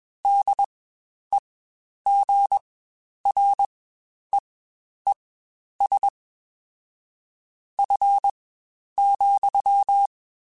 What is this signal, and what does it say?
Morse code, heard as a keyed tone, about 21 wpm, spaced as for 10 wpm, 800 Hz. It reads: DEGREES F,
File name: Spaced Characters / Spaced Characters-mp3s / Weather 3 17 sp @20